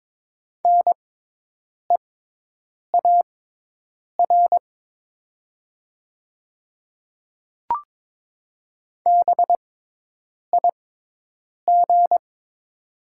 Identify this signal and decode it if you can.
Morse code, heard as a keyed tone, about 22 wpm, spaced as for 8 wpm, 700 Hz. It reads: NEAR BIG
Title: 6x Character Spacing - Top 300 Words - 22wpm